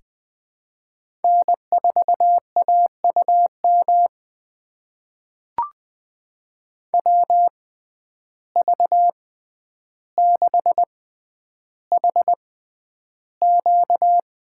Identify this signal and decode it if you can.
Morse code, heard as a keyed tone, about 20 wpm, spaced as for 10 wpm, 700 Hz. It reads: N4AUM WV6HQ
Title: USA Call Signs - 5 Characters - 6x Character Spacing - 20wpm